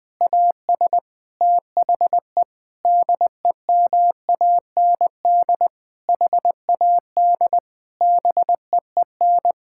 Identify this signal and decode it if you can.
Morse code, heard as a keyed tone, about 20 wpm, 700 Hz. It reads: AS THE DEMAND HAD BEEN
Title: Sentences from Top 1000 Words - Review Part 6 - 20wpm